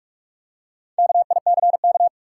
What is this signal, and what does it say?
Morse code, heard as a keyed tone, about 44 wpm, 700 Hz. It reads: KICK
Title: Four Letter Words - 45wpm